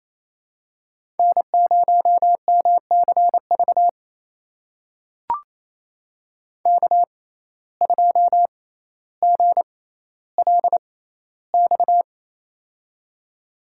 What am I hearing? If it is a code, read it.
Morse code, heard as a keyed tone, about 27 wpm, spaced as for 14 wpm, 700 Hz. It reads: N0MCV K2GLX
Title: USA Call Signs - 5 Characters - 6x Character Spacing - 28wpm